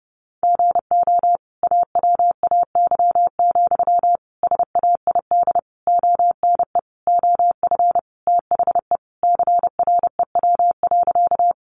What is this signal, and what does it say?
Morse code, heard as a keyed tone, about 28 wpm, 700 Hz. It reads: GO AWAY, SAID ONE OF THE CREW.
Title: Sentences from Top 1500 Words 30wpm